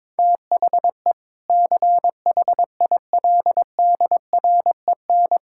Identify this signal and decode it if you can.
Morse code, heard as a keyed tone, about 22 wpm, 700 Hz. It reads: THE CHILDREN
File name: practice / 22wpm / Sentences with 2-Word Phrases - 22wpm